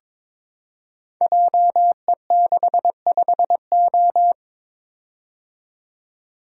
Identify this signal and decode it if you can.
Morse code, heard as a keyed tone, about 22 wpm, 700 Hz. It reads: JE65O
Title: Code Groups - 5 Letter:Numbers - 22wpm